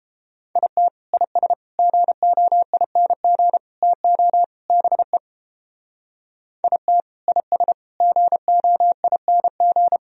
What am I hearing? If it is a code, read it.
Morse code, heard as a keyed tone, about 32 wpm, 700 Hz. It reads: IT IS GOING TO BE IT IS GOING